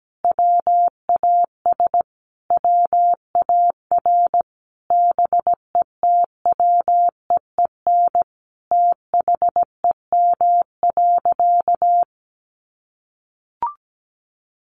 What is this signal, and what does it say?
Morse code, heard as a keyed tone, about 17 wpm, 700 Hz. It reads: WAS WAR BETWEEN THEM.
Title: Sentences from Top 400 Words 17wpm